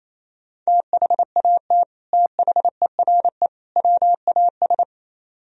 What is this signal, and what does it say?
Morse code, heard as a keyed tone, about 36 wpm, 700 Hz. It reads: THAT THERE WAS